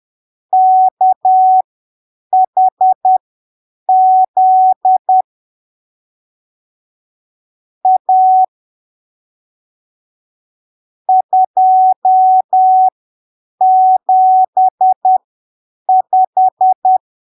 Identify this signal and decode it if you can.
Morse code, heard as a keyed tone, about 10 wpm, spaced as for 7 wpm, 750 Hz. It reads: KHZ A 275
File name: arquivos / texto / texto 12 10ppm